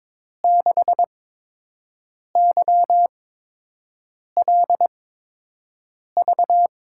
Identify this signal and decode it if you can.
Morse code, heard as a keyed tone, about 22 wpm, spaced as for 6 wpm, 700 Hz. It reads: 6YLV
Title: USA Call Signs - 8x Character Spacing - 22wpm